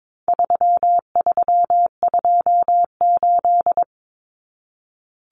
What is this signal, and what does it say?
Morse code, heard as a keyed tone, about 22 wpm, 700 Hz. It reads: 3328
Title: Common Cut Numbers - 4-Digit 22wpm